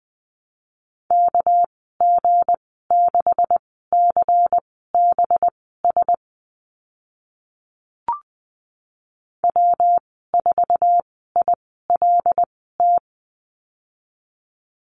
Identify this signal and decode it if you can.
Morse code, heard as a keyed tone, about 20 wpm, spaced as for 14 wpm, 700 Hz. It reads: KG6CBS W4ILT
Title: 207 Lesson 33g - Letter j - Call Signs 20wpm